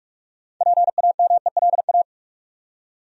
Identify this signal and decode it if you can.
Morse code, heard as a keyed tone, about 46 wpm, 700 Hz. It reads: PAMELA
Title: USA Common Names - 45wpm